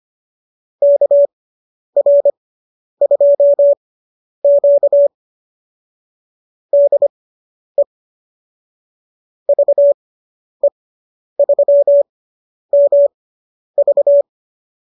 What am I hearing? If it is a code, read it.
Morse code, heard as a keyed tone, about 25 wpm, spaced as for 10 wpm, 570 Hz. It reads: KR2Q DE VE3MV